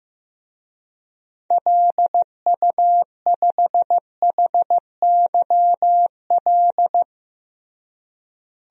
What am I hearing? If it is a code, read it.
Morse code, heard as a keyed tone, about 15 wpm, 700 Hz. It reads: LU5HYL
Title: DXCC-100 Call Signs - 6 Characters - Repeated 2x - 15wpm